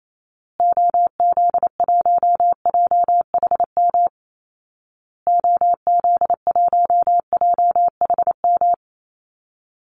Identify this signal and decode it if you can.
Morse code, heard as a keyed tone, about 28 wpm, 700 Hz. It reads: OZ1JHM OZ1JHM
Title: Call Signs Repeated 2x - 28wpm